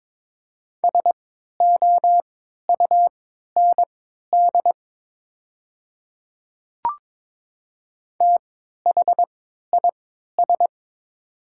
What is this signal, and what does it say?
Morse code, heard as a keyed tone, about 22 wpm, spaced as for 13 wpm, 700 Hz. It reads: SOUND THIS
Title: 3x Character Spacing - Top 200 Words - 22wpm